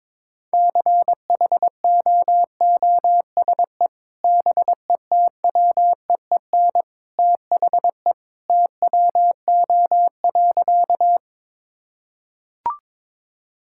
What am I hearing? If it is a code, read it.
Morse code, heard as a keyed tone, about 22 wpm, 700 Hz. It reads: CHOOSE BETWEEN THE TWO.
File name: practice / 22wpm / Sentences with 2-Word Phrases - 22wpm